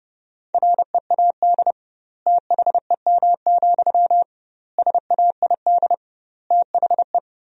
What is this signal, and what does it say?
Morse code, heard as a keyed tone, about 30 wpm, 700 Hz. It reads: READ THEM, SAID THE
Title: 2x Word-Spacing - Sentences from Top 400 Words - 30wpm